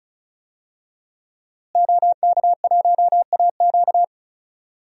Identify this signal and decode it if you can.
Morse code, heard as a keyed tone, about 36 wpm, 700 Hz. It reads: OK1AQ
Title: DXCC-100 Call Signs - 5 Characters - Repeated 2x - 35wpm